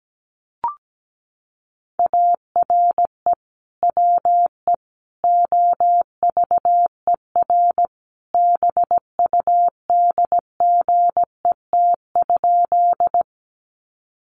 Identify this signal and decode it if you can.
Morse code, heard as a keyed tone, about 17 wpm, 700 Hz. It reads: ARE WE OVER BUDGET?
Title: Sentences from Top 1200 Words 17wpm